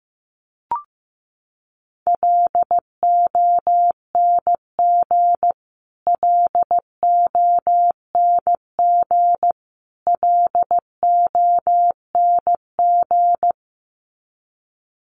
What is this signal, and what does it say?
Morse code, heard as a keyed tone, about 15 wpm, 700 Hz. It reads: LONG LONG LONG
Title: Top 500 Words Repeated 15wpm